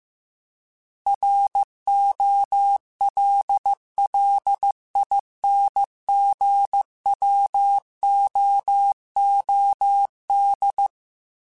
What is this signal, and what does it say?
Morse code, heard as a keyed tone, about 15 wpm, 800 Hz. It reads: ROLLINGWOOD